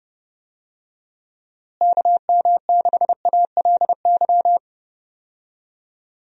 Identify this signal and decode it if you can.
Morse code, heard as a keyed tone, about 32 wpm, 700 Hz. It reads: KM6ALY
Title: USA Call Signs - 6 Characters - Repeated 2x - 30wpm